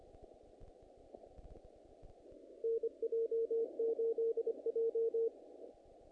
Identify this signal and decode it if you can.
Morse code, heard as a keyed tone, about 26 wpm, 460 Hz. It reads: NJ8J
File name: contests / naqp-cw-k5tr-op-k5pi- -2000